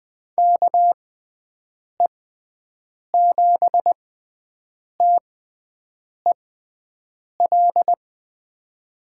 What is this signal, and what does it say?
Morse code, heard as a keyed tone, about 20 wpm, spaced as for 7 wpm, 700 Hz. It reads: KE7TEL
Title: USA Call Signs - 6x Character Spacing - 20wpm